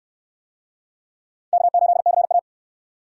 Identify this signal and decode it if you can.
Morse code, heard as a keyed tone, about 60 wpm, 700 Hz. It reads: K8CA